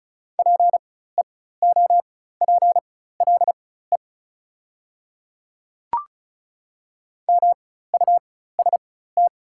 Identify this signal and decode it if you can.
Morse code, heard as a keyed tone, about 33 wpm, spaced as for 16 wpm, 690 Hz. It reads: PEOPLE MUST
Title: 4x Character Spacing - Top 300 Words - 35wpm